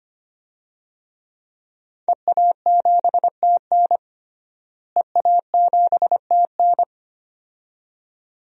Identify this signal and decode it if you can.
Morse code, heard as a keyed tone, about 25 wpm, 700 Hz. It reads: EA7TN EA7TN
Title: DXCC-100 Call Signs - 5 Characters - Repeated 2x - 25wpm